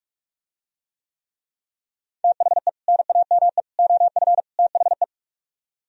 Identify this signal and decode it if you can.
Morse code, heard as a keyed tone, about 46 wpm, 700 Hz. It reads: THE NAME OF THE